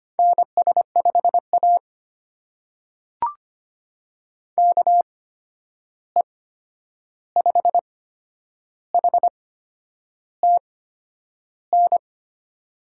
Morse code, heard as a keyed tone, about 25 wpm, spaced as for 9 wpm, 700 Hz. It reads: NS5A KE5HTN
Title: USA Call Signs - 8x Character Spacing - 25wpm